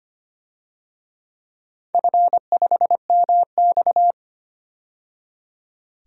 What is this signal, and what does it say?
Morse code, heard as a keyed tone, about 25 wpm, 700 Hz. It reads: F5MX
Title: DXCC-100 Call Signs - 4 Characters - 25wpm